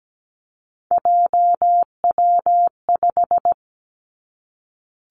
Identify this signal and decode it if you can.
Morse code, heard as a keyed tone, about 17 wpm, 700 Hz. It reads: JW5
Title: Three Challenging Letter-Number Combinations 17wpm